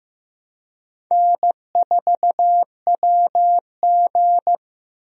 Morse code, heard as a keyed tone, about 16 wpm, 700 Hz. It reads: N4WG
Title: USA Call Signs - 4 Characters - 15wpm